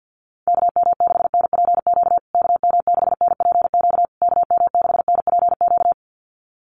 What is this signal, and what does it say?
Morse code, heard as a keyed tone, about 50 wpm, 700 Hz. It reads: KM6NPQ KM6NPQ KM6NPQ
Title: USA Call Signs Repeated 3x 50wpm